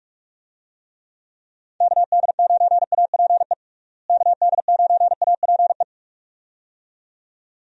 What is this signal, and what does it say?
Morse code, heard as a keyed tone, about 46 wpm, 700 Hz. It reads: KD9APE KD9APE
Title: USA Call Signs - 6 Characters - Repeated 2x - 45wpm